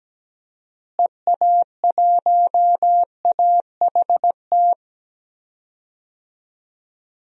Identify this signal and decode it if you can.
Morse code, heard as a keyed tone, about 17 wpm, 700 Hz. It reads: EA1AHT
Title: DXCC-100 Call Signs - 6 Characters - Repeated 2x - 17wpm